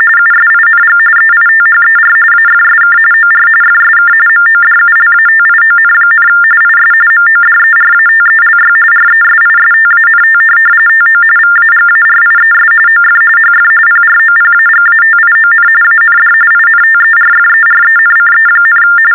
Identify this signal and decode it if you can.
TTY/TDD text telephone tones, about 45.45 baud. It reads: USING EITHER AFSK OR FSK MODULATION, IS MODERATELY RESISTANT TO VAGARIES OF HF PROPAGATION AND INTERFERENCE,
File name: teletype-RTTY-45